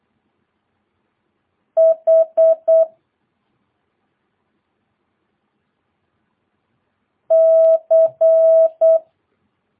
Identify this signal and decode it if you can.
Morse code, heard as a keyed tone, about 8 wpm, 650 Hz. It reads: H C
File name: jh8cbh / 2022cw